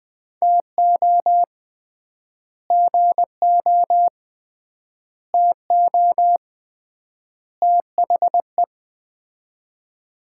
Morse code, heard as a keyed tone, about 20 wpm, 700 Hz. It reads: TO GO TO THE
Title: 3x Word-Spacing - Sets of 4 Words - 20wpm